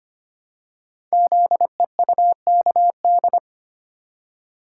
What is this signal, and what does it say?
Morse code, heard as a keyed tone, about 25 wpm, 700 Hz. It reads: ZEUKD
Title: Code Groups - 5 Letters - 25wpm